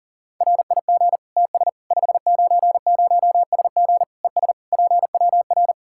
Morse code, heard as a keyed tone, about 40 wpm, 700 Hz. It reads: RIG TS 590SG ES PWR
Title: Practice QSO - Exchange 2 - Info - 40wpm